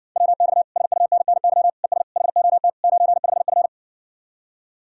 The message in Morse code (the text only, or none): WX SUNNY ES HOT 95F